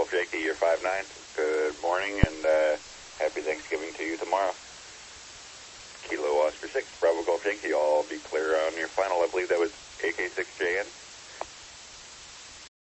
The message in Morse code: none

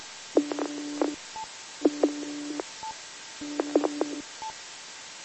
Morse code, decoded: none

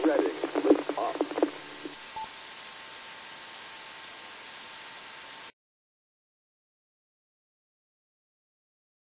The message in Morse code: none

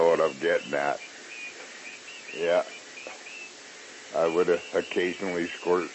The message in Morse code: none

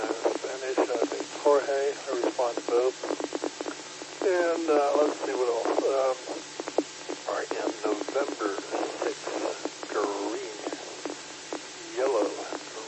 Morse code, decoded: none

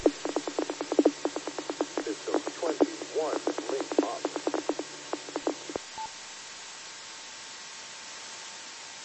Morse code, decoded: none

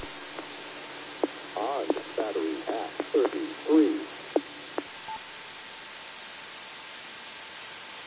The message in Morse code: none

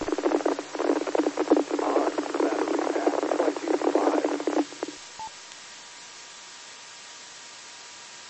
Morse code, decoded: none

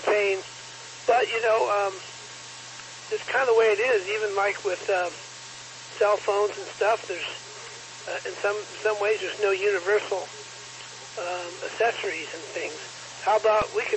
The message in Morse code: none

